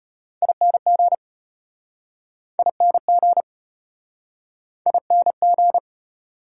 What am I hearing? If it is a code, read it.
Morse code, heard as a keyed tone, about 34 wpm, 700 Hz. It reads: ING ING ING